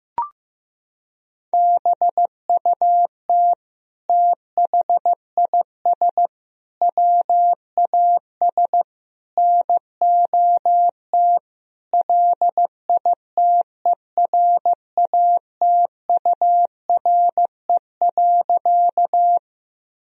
Morse code, heard as a keyed tone, about 15 wpm, 700 Hz. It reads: BUT THIS WAS NOT LITERATURE.
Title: Sentences from Top 1000 Words - Review Part 3 - 15wpm